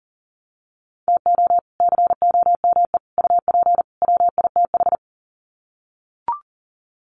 Morse code, heard as a keyed tone, about 38 wpm, 700 Hz. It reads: TO COME UP WITH
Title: Sets of 4 Words 40wpm